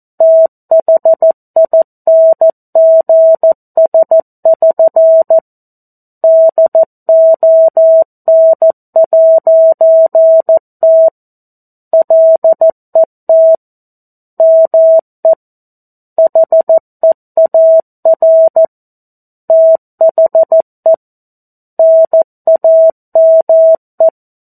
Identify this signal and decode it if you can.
Morse code, heard as a keyed tone, about 14 wpm, 650 Hz. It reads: THINGS<SN> DON'T LET ME HEAR THE NAME
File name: 02-Chapter-02 14 WPM